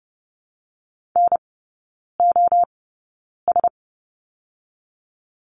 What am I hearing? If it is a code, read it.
Morse code, heard as a keyed tone, about 30 wpm, 700 Hz. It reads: N O S
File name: Lesson 19e - Letter w - Three Characters 30wpm